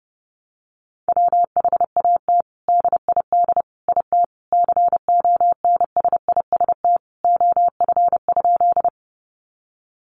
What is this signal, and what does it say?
Morse code, heard as a keyed tone, about 30 wpm, 700 Hz. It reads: WHAT DID IT CONSIST OF?
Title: Sentences from Top 1400 Words 30wpm